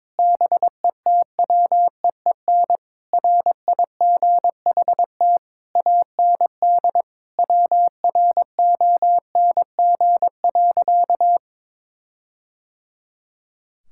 Morse code, heard as a keyed tone, about 22 wpm, 700 Hz. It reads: BETWEEN RIGHT AND WRONG.
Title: Sentences from Top 1600 Words 22wpm